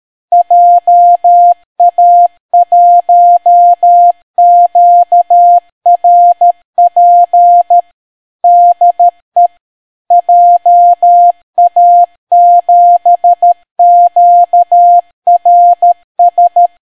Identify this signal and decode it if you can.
Morse code, heard as a keyed tone, about 13 wpm, 700 Hz. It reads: JA1QRP DE JA7QRS